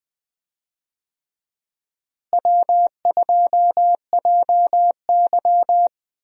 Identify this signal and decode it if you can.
Morse code, heard as a keyed tone, about 20 wpm, 700 Hz. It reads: W2JY
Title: USA Call Signs - 4 Characters - Repeated 3x - 20wpm